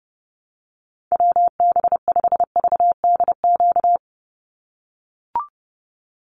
Morse code, heard as a keyed tone, about 30 wpm, 700 Hz. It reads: WB5VDQ
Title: USA Call Signs 30wpm